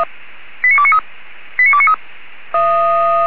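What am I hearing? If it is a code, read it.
Five-tone selective call (ZVEI), heard as quick runs of five tones, tones of about 70 ms. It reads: 89293 89293
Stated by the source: Five-tone ZVEI selective call 89288 received on an SDR, from the feuerwehr-orlamuende.org archive.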